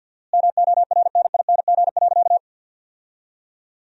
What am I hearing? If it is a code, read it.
Morse code, heard as a keyed tone, about 51 wpm, 700 Hz. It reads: MORNING.
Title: 4x Word-Spacing - Sentences from Top 400 Words - 50wpm